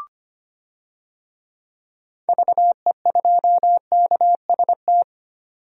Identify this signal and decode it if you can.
Morse code, heard as a keyed tone, about 25 wpm, 700 Hz. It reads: VE2KST
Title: DXCC-100 Call Signs - 6 Characters - 25wpm